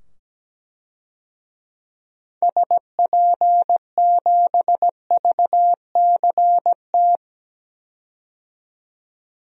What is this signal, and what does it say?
Morse code, heard as a keyed tone, about 17 wpm, 700 Hz. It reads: SP7VCT
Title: DXCC-100 Call Signs - 6 Characters - 17wpm